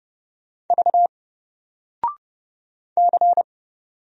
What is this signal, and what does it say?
Morse code, heard as a keyed tone, about 30 wpm, 700 Hz. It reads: V C
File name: Single-Letter-Number-30wpm